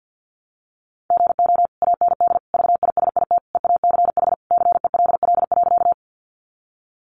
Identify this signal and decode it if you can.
Morse code, heard as a keyed tone, about 50 wpm, 700 Hz. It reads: GO AND VISIT EACH CELL.
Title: Sentences from Top 1100 Words 50wpm